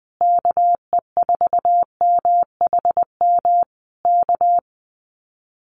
Morse code, heard as a keyed tone, about 20 wpm, 700 Hz. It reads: KE4MHM K
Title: Common QSO Phrases Encore 20wpm